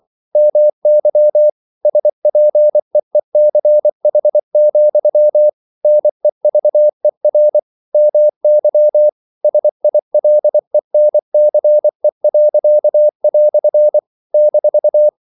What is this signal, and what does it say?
Morse code, heard as a keyed tone, about 24 wpm, 600 Hz. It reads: MY SPEECH, NEVER MY SILENCE." -
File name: CW / Practice / EnglishQuotations0224